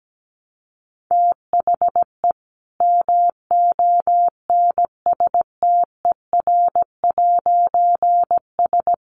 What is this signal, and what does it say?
Morse code, heard as a keyed tone, about 17 wpm, 700 Hz. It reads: THE MONSTER'S
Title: A Sound Of Thunder 17wpm